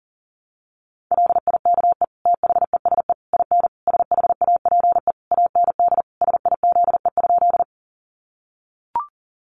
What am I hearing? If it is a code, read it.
Morse code, heard as a keyed tone, about 40 wpm, 700 Hz. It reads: LIKE THESE IN SHAPE AND SIZE?
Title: Sentences from Top 500 Words 40wpm